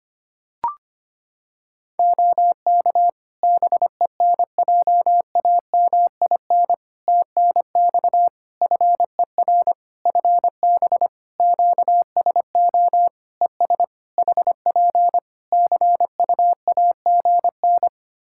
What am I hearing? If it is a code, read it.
Morse code, heard as a keyed tone, about 25 wpm, 700 Hz. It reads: OK BENJAMIN TNX FER FB QSO ES HP CUAGN
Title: K1USN SST QSO Practice 25wpm